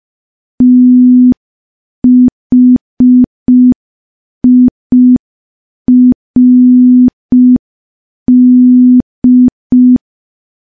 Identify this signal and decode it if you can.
Morse code, heard as a keyed tone, about 5 wpm, 250 Hz. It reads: THIRD
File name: morse / morse-training-05WPM